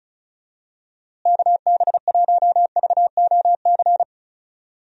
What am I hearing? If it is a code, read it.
Morse code, heard as a keyed tone, about 36 wpm, 700 Hz. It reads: KB1VOC
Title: USA Call Signs - 6 Characters - Repeated 2x - 35wpm